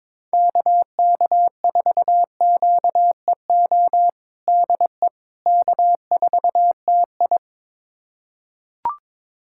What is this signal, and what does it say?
Morse code, heard as a keyed tone, about 21 wpm, 700 Hz. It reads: KK4QEO DE K4TI
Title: USA Callsign DE Callsign 22wpm